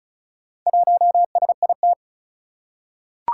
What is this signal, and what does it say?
Morse code, heard as a keyed tone, about 36 wpm, 700 Hz. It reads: 1SIT
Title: Four Challenging Letter-Number Combinations 35wpm